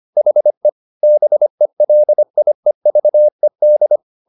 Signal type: Morse code